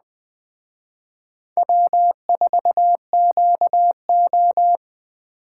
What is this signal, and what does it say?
Morse code, heard as a keyed tone, about 20 wpm, 700 Hz. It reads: W4QO
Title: Speed Racing - USA 1x2 Call Signs 17wpm